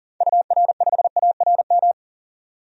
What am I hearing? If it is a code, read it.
Morse code, heard as a keyed tone, about 40 wpm, 700 Hz. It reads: UR5ARM
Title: DXCC-100 Call Signs - 6 Characters - 40wpm